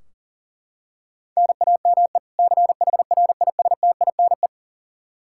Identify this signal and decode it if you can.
Morse code, heard as a keyed tone, about 40 wpm, 700 Hz. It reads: NAME CHRISTINE
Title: Practice QSO - Exchange 1 - Beginning - 40wpm